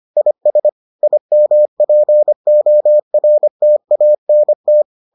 Morse code, heard as a keyed tone, about 25 wpm, 600 Hz. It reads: IS IMPORTANT